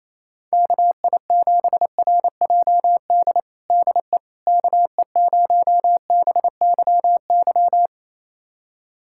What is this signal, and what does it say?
Morse code, heard as a keyed tone, about 28 wpm, 700 Hz. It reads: KI7RJD DE KE0BYY
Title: USA Callsign DE Callsign - 6 Characters - 28wpm